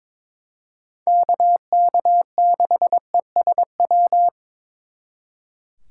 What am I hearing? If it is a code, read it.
Morse code, heard as a keyed tone, about 26 wpm, 700 Hz. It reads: KK6ESW
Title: Speed Racing - USA 2x3 Call Signs 22wpm